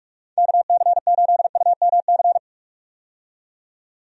Morse code, heard as a keyed tone, about 46 wpm, 700 Hz. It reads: KC8UMC
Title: USA Call Signs - 6 Characters - 45wpm